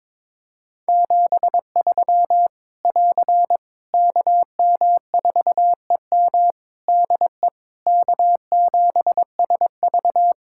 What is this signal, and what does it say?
Morse code, heard as a keyed tone, about 22 wpm, 700 Hz. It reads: 73 <AR> KM4EM DE K7SV
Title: Practice QSO - Exchange 4 - Ending - 22wpm